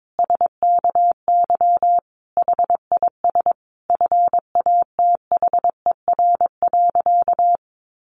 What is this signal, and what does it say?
Morse code, heard as a keyed tone, about 22 wpm, 700 Hz. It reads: SKY HIS FATHER.
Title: Sentences from Top 1200 Words 22wpm